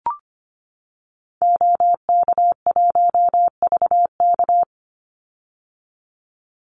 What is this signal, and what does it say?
Morse code, heard as a keyed tone, about 25 wpm, 700 Hz. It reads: OK1VK
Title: Call Signs - 25wpm